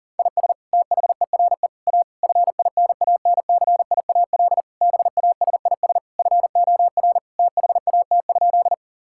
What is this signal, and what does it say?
Morse code, heard as a keyed tone, about 40 wpm, 700 Hz. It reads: IS THERE A FINANCIAL BASIS FOR THAT?
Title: Sentences from Top 1000 Words - Review Part 5 - 40wpm